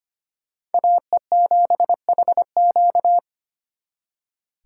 Morse code, heard as a keyed tone, about 32 wpm, 700 Hz. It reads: AE7HQ